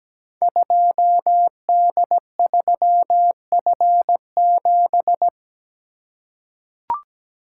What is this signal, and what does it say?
Morse code, heard as a keyed tone, about 17 wpm, 700 Hz. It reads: 2D3F7
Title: Code Groups - 5 Letter:Numbers - 17wpm